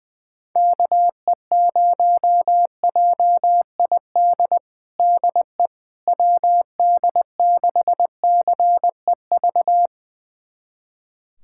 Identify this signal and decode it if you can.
Morse code, heard as a keyed tone, about 20 wpm, 700 Hz. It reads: KE0JID DE WD6CEV
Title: USA Callsign DE Callsign - 6 Characters - 20wpm